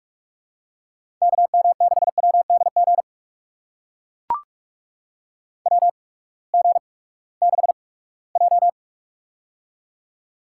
Morse code, heard as a keyed tone, about 44 wpm, spaced as for 22 wpm, 700 Hz. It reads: KM6WDG WG6J
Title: USA Call Signs - 8x Character Spacing - 45wpm